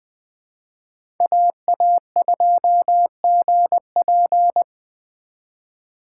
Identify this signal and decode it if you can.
Morse code, heard as a keyed tone, about 20 wpm, 700 Hz. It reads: AA2GP
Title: USA Call Signs - 4x Character Spacing - 20wpm